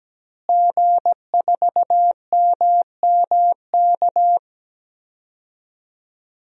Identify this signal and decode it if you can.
Morse code, heard as a keyed tone, about 17 wpm, 700 Hz. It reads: G4MMK
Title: DXCC-100 Call Signs - 5 Characters - Repeated 3x - 17wpm